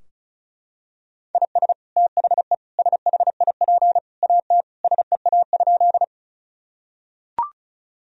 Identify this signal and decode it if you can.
Morse code, heard as a keyed tone, about 34 wpm, 700 Hz. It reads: IS THE SHIP AT SEA?
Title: Sentences from Top 1000 Words - Review Part 2 - 35wpm